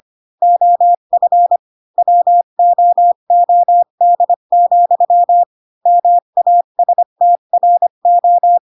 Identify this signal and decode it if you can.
Morse code, heard as a keyed tone, about 25 wpm, 700 Hz. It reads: OF WOOD, MASTRO
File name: Pinocchio / Pinocchio ch1 20 25 ik1hge